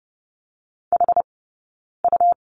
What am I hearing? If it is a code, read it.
Morse code, heard as a keyed tone, about 30 wpm, 700 Hz. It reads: H U
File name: Lesson 19d - Letter w - Two Characters 30wpm